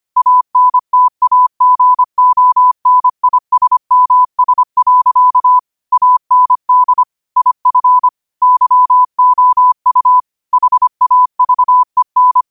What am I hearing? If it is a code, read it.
Morse code, heard as a keyed tone, about 25 wpm, 1000 Hz. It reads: ANTAGONISMS. AND IF YOU HAVEN